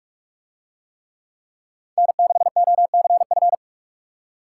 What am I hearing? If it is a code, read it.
Morse code, heard as a keyed tone, about 44 wpm, 700 Hz. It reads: N6OCF